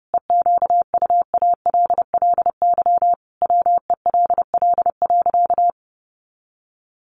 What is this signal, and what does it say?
Morse code, heard as a keyed tone, about 28 wpm, 700 Hz. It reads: EQUALLY WELL.
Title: Sentences from Top 1400 Words 30wpm